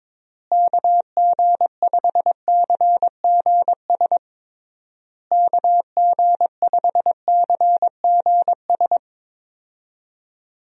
Morse code, heard as a keyed tone, about 22 wpm, 700 Hz. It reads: KG5CGS KG5CGS
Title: USA Call Signs - 6 Characters - Repeated 3x - 22wpm